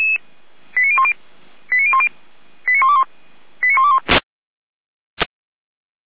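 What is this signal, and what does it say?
Five-tone selective call (ZVEI), heard as quick runs of five tones, tones of about 70 ms. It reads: 89010 89010 89121 89121
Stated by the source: ZVEI five-tone call 89119 received via SDR.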